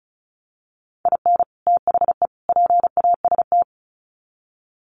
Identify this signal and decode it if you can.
Morse code, heard as a keyed tone, about 34 wpm, 700 Hz. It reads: IN THE PAST
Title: Sets of 3 Words 35wpm